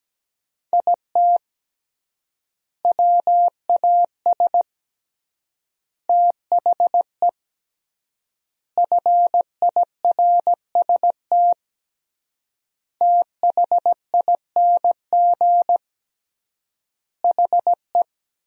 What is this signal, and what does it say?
Morse code, heard as a keyed tone, about 17 wpm, 700 Hz. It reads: IT WAS THE FIRST THING HE
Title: 3x Word-Spacing - Sentences from Top 300 Words - 17wpm